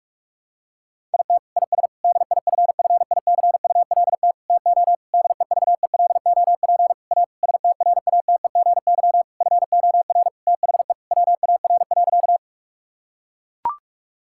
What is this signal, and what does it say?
Morse code, heard as a keyed tone, about 41 wpm, 700 Hz. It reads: IT IS DIFFICULT TO DEVELOP A STRATEGY FOR THE WAR.
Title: Sentences from Top 1000 Words - Review Part 5 - 45wpm